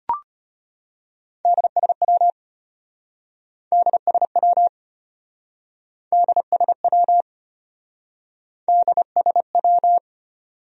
Morse code, heard as a keyed tone, about 35 wpm, 700 Hz. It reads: DSW DSW DSW DSW